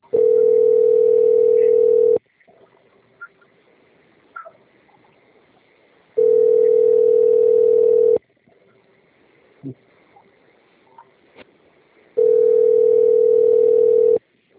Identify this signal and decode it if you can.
Ringback tone (North America), 440 + 480 Hz.